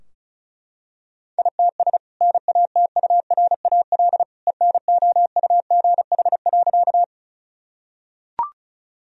Morse code, heard as a keyed tone, about 36 wpm, 700 Hz. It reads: ITS NATURAL ENOUGH.
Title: Sentences from Top 1000 Words - Review Part 4 - 35wpm